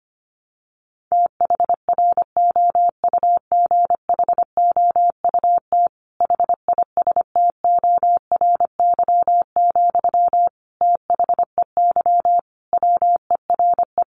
Morse code, heard as a keyed tone, about 25 wpm, 700 Hz. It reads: THROUGHOUT HISTORY, THEY WERE
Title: Sentences from Top 800 Words 25wpm